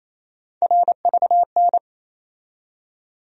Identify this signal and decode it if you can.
Morse code, heard as a keyed tone, about 28 wpm, 700 Hz. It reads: RVN
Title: USA Call Sign Postfixes - 3 Letters 28wpm